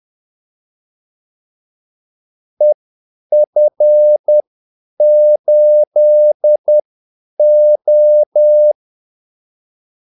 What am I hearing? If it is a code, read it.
Morse code, heard as a keyed tone, about 10 wpm, spaced as for 8 wpm, 600 Hz. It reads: EF8O
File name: Callsigns-10-8wpm